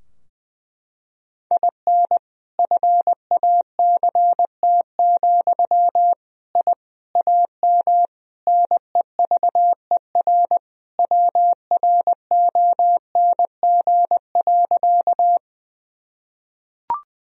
Morse code, heard as a keyed tone, about 20 wpm, 700 Hz. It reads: IN FACT, I AM NEVER WRONG.
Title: Sentences from Top 1000 Words - Review Part 3 - 20wpm